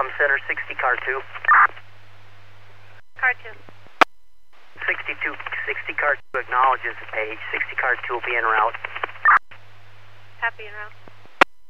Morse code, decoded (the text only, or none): none